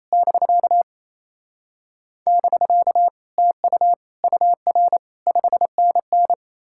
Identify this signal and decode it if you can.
Morse code, heard as a keyed tone, about 38 wpm, spaced as for 33 wpm, 700 Hz. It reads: <BK> <BK> TU UR 5NN
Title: Speed-Racing Common QSO Phrases - 28wpm